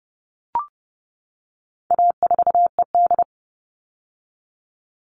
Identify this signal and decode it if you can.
Morse code, heard as a keyed tone, about 30 wpm, 700 Hz. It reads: A4ED
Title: All Cut Numbers - 4-Digit 30wpm